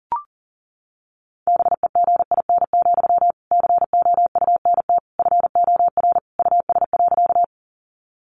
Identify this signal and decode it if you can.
Morse code, heard as a keyed tone, about 40 wpm, 700 Hz. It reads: BEGIN, COUNT FOR US.
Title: Sentences from Top 1500 Words 40wpm